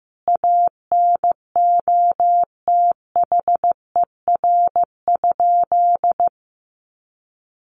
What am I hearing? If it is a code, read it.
Morse code, heard as a keyed tone, about 15 wpm, 700 Hz. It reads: ANOTHER?
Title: Sentences from Top 1400 Words 15wpm